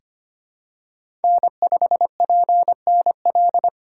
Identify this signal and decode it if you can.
Morse code, heard as a keyed tone, about 25 wpm, 700 Hz. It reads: N5PNL
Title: USA Call Signs - 5 Characters - 8x Character Spacing - 25wpm